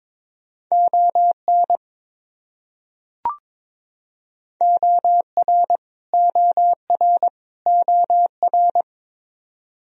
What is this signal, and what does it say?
Morse code, heard as a keyed tone, about 22 wpm, 700 Hz. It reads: ON OR OR OR
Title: Top 100 Words Repeated 22wpm